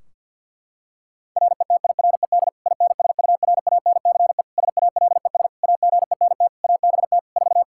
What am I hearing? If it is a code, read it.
Morse code, heard as a keyed tone, about 53 wpm, 700 Hz. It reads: RETIRED INSURANCE SALES AGENT ABT 4
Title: K1USN SST QSO Practice 50wpm